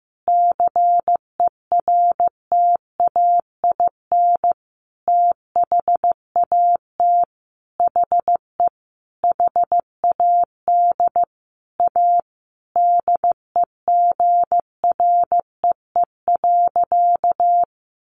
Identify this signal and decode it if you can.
Morse code, heard as a keyed tone, about 15 wpm, 700 Hz. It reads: CERTAIN THAT HE HAD A DEGREE.
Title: Sentences from Top 1500 Words 15wpm